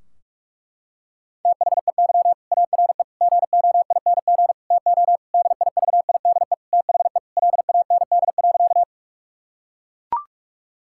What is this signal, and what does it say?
Morse code, heard as a keyed tone, about 41 wpm, 700 Hz. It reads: THEY ARE GOING TO DIVIDE THE LAND.
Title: Sentences from Top 1600 Words 45wpm